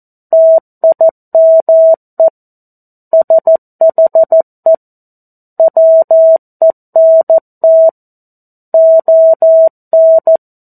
Morse code, heard as a keyed tone, about 14 wpm, 650 Hz. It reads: TIME SHE WENT ON
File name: 02-Chapter-02 14 WPM